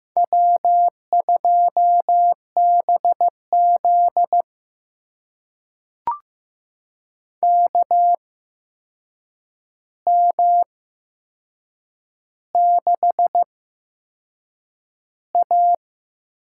Morse code, heard as a keyed tone, about 15 wpm, spaced as for 6 wpm, 700 Hz. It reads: W2BZ KM6A
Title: USA Call Signs - 4 Characters - 8x Character Spacing - 15wpm